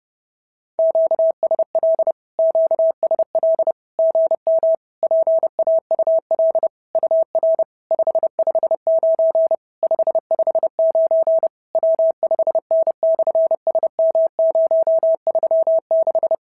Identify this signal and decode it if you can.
Morse code, heard as a keyed tone, about 30 wpm, 650 Hz. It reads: QSL QSL GM PAUL UR 559 559 W5N/SM036
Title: SOTA QSO Practice 30wpm